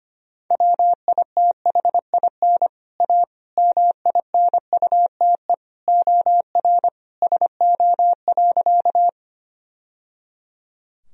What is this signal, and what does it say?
Morse code, heard as a keyed tone, about 25 wpm, 700 Hz. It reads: WITHIN A MINUTE OR SO.
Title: Sentences from Top 1000 Words - Review Part 4 - 25wpm